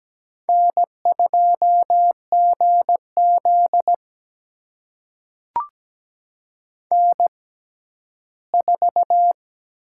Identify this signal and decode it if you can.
Morse code, heard as a keyed tone, about 17 wpm, spaced as for 11 wpm, 700 Hz. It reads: N2GZ N4
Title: USA Call Signs - 4 Characters - 6x Character Spacing - 17wpm